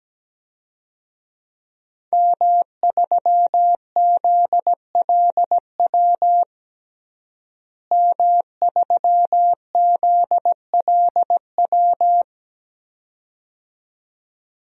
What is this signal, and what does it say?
Morse code, heard as a keyed tone, about 17 wpm, 700 Hz. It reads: M3ZLW M3ZLW
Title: DXCC-100 Call Signs - 5 Characters - Repeated 2x - 17wpm